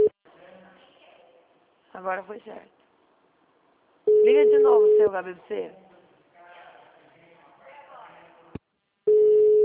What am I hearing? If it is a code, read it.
Ringback tone (Europe, 425 Hz), with speech.